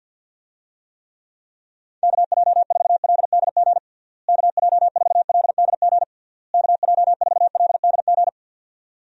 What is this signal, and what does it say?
Morse code, heard as a keyed tone, about 52 wpm, 700 Hz. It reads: KJ4LDG KJ4LDG KJ4LDG
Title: USA Call Signs - 6 Characters - Repeated 3x - 50wpm